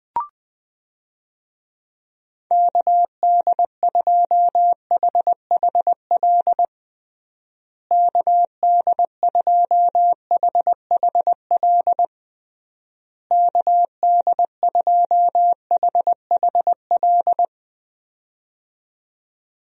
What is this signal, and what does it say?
Morse code, heard as a keyed tone, about 21 wpm, 700 Hz. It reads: KD2HHL KD2HHL KD2HHL
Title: USA Call Signs - 6 Characters - Repeated 3x - 20wpm